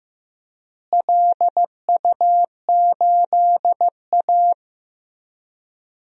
Morse code, heard as a keyed tone, about 15 wpm, 700 Hz. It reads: LU8A
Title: DXCC-100 Call Signs - 4 Characters - Repeated 3x - 15wpm